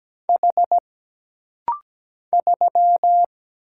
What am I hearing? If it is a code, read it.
Morse code, heard as a keyed tone, about 17 wpm, 700 Hz. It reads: H 3
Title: Single Letter-Number Alt - Rapid-Fire 17wpm